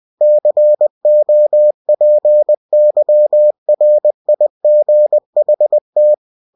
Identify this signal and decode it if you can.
Morse code, heard as a keyed tone, about 20 wpm, 600 Hz. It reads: COPYRIGHT
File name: cw-books / winnie-the-pooh / winnie-20wpm-page-0114